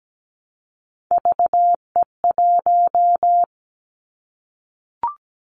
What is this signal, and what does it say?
Morse code, heard as a keyed tone, about 17 wpm, 700 Hz. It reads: VE1
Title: Callsign Prefixes for Top 100 DXCC - 17wpm